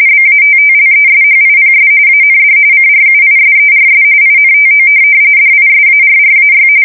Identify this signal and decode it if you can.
RTTY, about 45.45 baud. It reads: BATTLESHIPS AND CRUISERS INTO DISARRAY
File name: fsk170 / USS-Massachusetts